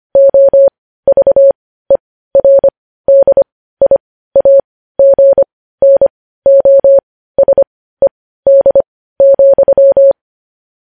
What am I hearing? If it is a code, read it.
Morse code, heard as a keyed tone, about 25 wpm, spaced as for 15 wpm, 550 Hz. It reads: OVERDIAGNOSED,
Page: news-headlines-in-morse-code-at-15-wpm